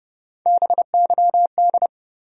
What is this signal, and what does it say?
Morse code, heard as a keyed tone, about 30 wpm, 700 Hz. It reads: BYD
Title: USA Call Sign Postfixes - 3 Letters 30wpm